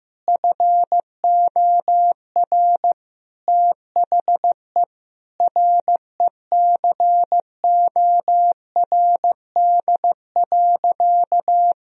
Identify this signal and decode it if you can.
Morse code, heard as a keyed tone, about 15 wpm, 700 Hz. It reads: FOR THE RECORD.
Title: Sentences from Top 1000 Words - Review Part 3 - 15wpm